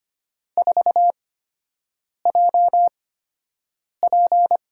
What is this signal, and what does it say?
Morse code, heard as a keyed tone, about 25 wpm, spaced as for 7 wpm, 700 Hz. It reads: 4JP
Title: USA Call Signs - 5 Characters - 8x Character Spacing - 25wpm